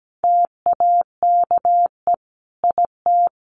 Morse code, heard as a keyed tone, about 17 wpm, 700 Hz. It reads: TAKE IT
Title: Sentences from Top 1500 Words 17wpm